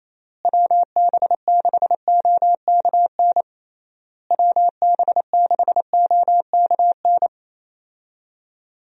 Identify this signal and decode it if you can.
Morse code, heard as a keyed tone, about 28 wpm, 700 Hz. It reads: WB6OKN WB6OKN
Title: USA Call Signs - 6 Characters - Repeated 3x - 28wpm